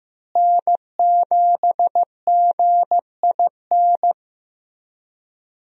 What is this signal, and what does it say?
Morse code, heard as a keyed tone, about 15 wpm, 700 Hz. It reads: N7GIN
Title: USA Call Signs - 5 Characters - Repeated 3x - 15wpm